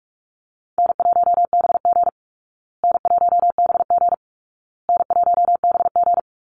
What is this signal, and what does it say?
Morse code, heard as a keyed tone, about 44 wpm, 700 Hz. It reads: N1BG N1BG N1BG
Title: Call Signs Repeated 2x - 45wpm